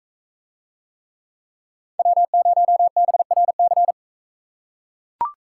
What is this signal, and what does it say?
Morse code, heard as a keyed tone, about 41 wpm, 700 Hz. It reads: W0BRC